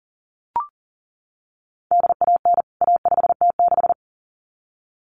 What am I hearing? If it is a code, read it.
Morse code, heard as a keyed tone, about 40 wpm, 700 Hz. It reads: DAN A5T6
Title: CWT Exchange 40wpm